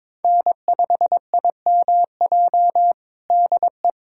Morse code, heard as a keyed tone, about 22 wpm, 700 Hz. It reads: N5IMJ DE
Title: USA Callsign DE Callsign - 5 Characters - 22wpm